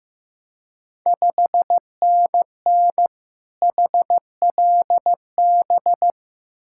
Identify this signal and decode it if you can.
Morse code, heard as a keyed tone, about 15 wpm, 690 Hz. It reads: 5NN HLB
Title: Ontario QSO Party Exchange 15wpm